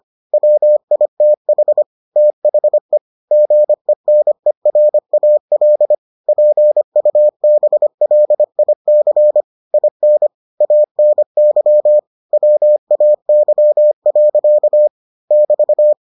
Morse code, heard as a keyed tone, about 25 wpm, 600 Hz. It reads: WITH THE GENERAL PUBLIC IN ANY WAY. <BT>